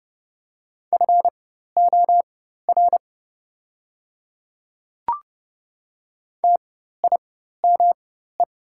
Morse code, heard as a keyed tone, about 30 wpm, spaced as for 14 wpm, 700 Hz. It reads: FOR TIME
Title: 4x Character Spacing - Top 200 Words - 30wpm